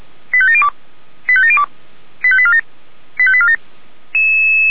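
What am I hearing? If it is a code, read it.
Five-tone selective call (ZVEI), heard as quick runs of five tones, tones of about 70 ms. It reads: 85702 85702 85847 85847
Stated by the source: SDR-received ZVEI five-tone selective call.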